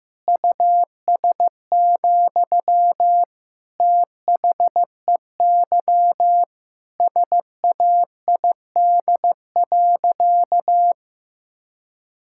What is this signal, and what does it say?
Morse code, heard as a keyed tone, about 15 wpm, 700 Hz. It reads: US, THEY SAID.
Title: Sentences from Top 1600 Words 15wpm